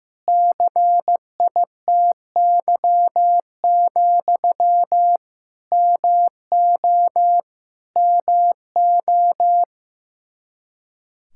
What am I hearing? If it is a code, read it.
Morse code, heard as a keyed tone, about 16 wpm, 700 Hz. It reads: CITY, MO MO
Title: Practice QSO - Exchange 1 - Beginning - 15wpm